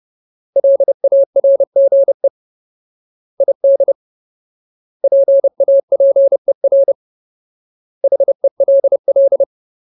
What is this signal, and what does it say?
Morse code, heard as a keyed tone, about 30 wpm, 550 Hz. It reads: LARGE ID PAPER HELL
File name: Ultimate Word List / 30WPM Ultimate